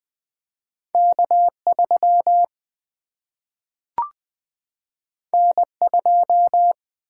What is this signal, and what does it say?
Morse code, heard as a keyed tone, about 20 wpm, 700 Hz. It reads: K 3 N 2
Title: Single-Letter Single-Number 20wpm